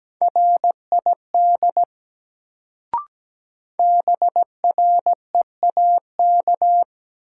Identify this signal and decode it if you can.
Morse code, heard as a keyed tone, about 17 wpm, 700 Hz. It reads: RID BREAK
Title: Sets of 4 Words - New Word Drill - 17wpm